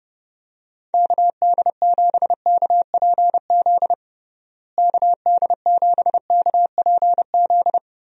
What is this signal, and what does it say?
Morse code, heard as a keyed tone, about 30 wpm, 700 Hz. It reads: KD7KPZ KD7KPZ
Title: USA Call Signs - 6 Characters - Repeated 2x - 30wpm